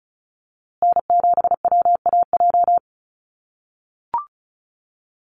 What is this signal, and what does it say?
Morse code, heard as a keyed tone, about 34 wpm, 700 Hz. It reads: N7WAJ
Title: USA Call Signs Repeated 2x 35wpm